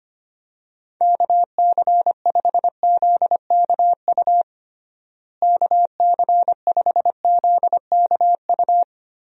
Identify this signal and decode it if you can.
Morse code, heard as a keyed tone, about 25 wpm, 700 Hz. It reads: KC5ZKU KC5ZKU
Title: USA Call Signs - 6 Characters - Repeated 3x - 25wpm